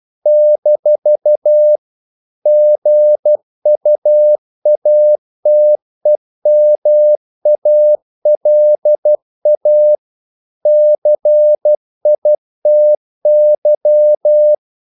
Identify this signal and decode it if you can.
Morse code, heard as a keyed tone, about 12 wpm, 600 Hz. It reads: - GUATEMALA CITY
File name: CW / Practice / NationalCapitals12